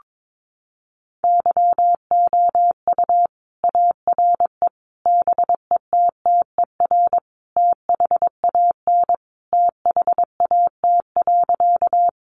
Morse code, heard as a keyed tone, about 22 wpm, 700 Hz. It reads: YOU ARE BETTER THAN THAT.
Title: Sentences from Top 400 Words 22wpm